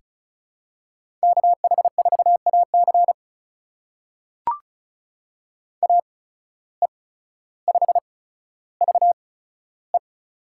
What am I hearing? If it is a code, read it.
Morse code, heard as a keyed tone, about 34 wpm, spaced as for 15 wpm, 700 Hz. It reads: KH4AC AE5VE
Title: USA Call Signs - 5 Characters - 8x Character Spacing - 35wpm